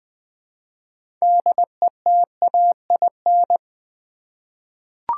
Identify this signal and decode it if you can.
Morse code, heard as a keyed tone, about 20 wpm, 700 Hz. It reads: DETAIN
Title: 6 Letter Words - First 3 Letters Given - 20wpm